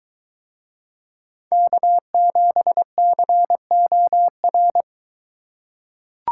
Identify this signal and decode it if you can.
Morse code, heard as a keyed tone, about 21 wpm, 700 Hz. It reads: K7COR